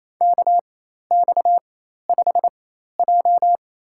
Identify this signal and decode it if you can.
Morse code, heard as a keyed tone, about 28 wpm, spaced as for 13 wpm, 700 Hz. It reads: KX5J
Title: USA Call Signs - 4x Character Spacing - 28wpm